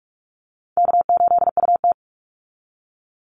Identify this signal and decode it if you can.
Morse code, heard as a keyed tone, about 44 wpm, 700 Hz. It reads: K8UT
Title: Call Signs Repeated 3x - 45wpm